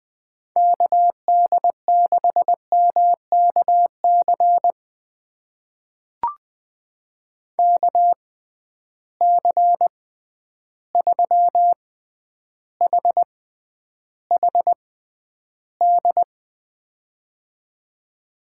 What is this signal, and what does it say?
Morse code, heard as a keyed tone, about 20 wpm, spaced as for 10 wpm, 700 Hz. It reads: KD6MKC KC3HHD
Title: USA Call Signs - 6 Characters - 6x Character Spacing - 20wpm